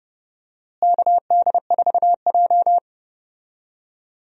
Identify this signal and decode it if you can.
Morse code, heard as a keyed tone, about 31 wpm, 700 Hz. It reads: KD4J
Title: USA Call Signs - 4 Characters - Repeated 2x - 30wpm